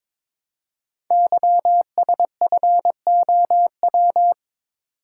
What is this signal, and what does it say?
Morse code, heard as a keyed tone, about 23 wpm, 700 Hz. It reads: YSFOW
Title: Code Groups - 5 Letter:Numbers - 22wpm